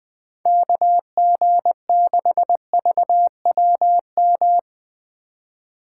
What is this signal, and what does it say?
Morse code, heard as a keyed tone, about 20 wpm, 700 Hz. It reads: KG6VWM
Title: USA Call Signs - 6 Characters - 4x Character Spacing - 20wpm